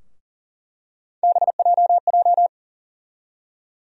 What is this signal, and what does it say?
Morse code, heard as a keyed tone, about 40 wpm, 700 Hz. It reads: BJJ
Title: USA Call Sign Postfixes - 3 Letters 40wpm